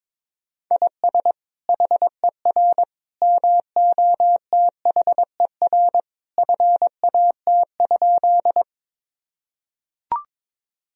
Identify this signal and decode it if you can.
Morse code, heard as a keyed tone, about 21 wpm, 700 Hz. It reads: IS HER MOTHER FAT?
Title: Sentences from Top 1000 Words - Review Part 4 - 22wpm